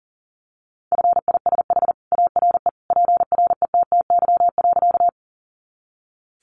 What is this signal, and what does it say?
Morse code, heard as a keyed tone, about 40 wpm, 700 Hz. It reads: FISH ARE PRETTY.
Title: Easy Phrases 40wpm